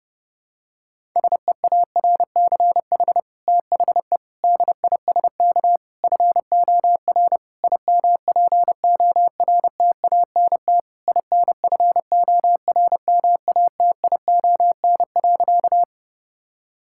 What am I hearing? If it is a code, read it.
Morse code, heard as a keyed tone, about 30 wpm, 700 Hz. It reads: SEARCH THE DISK FOR IMPORTANT INFORMATION.
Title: Sentences from Top 1000 Words - Review Part 6 - 30wpm